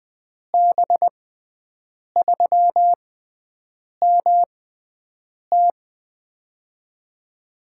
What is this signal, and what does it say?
Morse code, heard as a keyed tone, about 21 wpm, spaced as for 7 wpm, 700 Hz. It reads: B3MT
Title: USA Call Signs - 6x Character Spacing - 20wpm